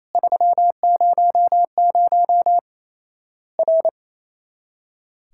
Morse code, heard as a keyed tone, about 28 wpm, 700 Hz. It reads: 300 R
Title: Worked All Europe DX Contest - QTC Exchange - 28wpm